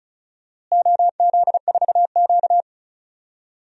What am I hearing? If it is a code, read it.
Morse code, heard as a keyed tone, about 36 wpm, 700 Hz. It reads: OZ4Q
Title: DXCC-100 Call Signs - 4 Characters - Repeated 3x - 35wpm